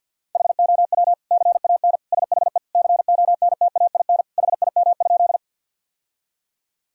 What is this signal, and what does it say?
Morse code, heard as a keyed tone, about 50 wpm, 700 Hz. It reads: HOW CAN SHE CONTAIN HIM?
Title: Sentences from Top 1000 Words - Review Part 2 - 50wpm